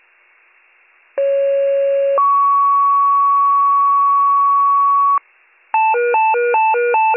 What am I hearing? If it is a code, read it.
Two-tone page, heard as a one-second tone, then a three-second tone, about 550 then 1120 Hz. It is heard from 1.2 to 5.2 s.